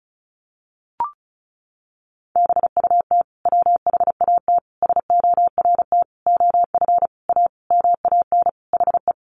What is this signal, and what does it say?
Morse code, heard as a keyed tone, about 34 wpm, 700 Hz. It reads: BUT WHAT SORT OF A MAN HE
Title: Sentences from Top 1300 Words 35wpm